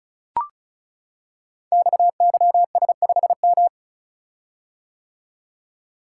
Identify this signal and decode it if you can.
Morse code, heard as a keyed tone, about 36 wpm, 700 Hz. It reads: XYS5M
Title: Code Groups - 5 Letter:Numbers - 35wpm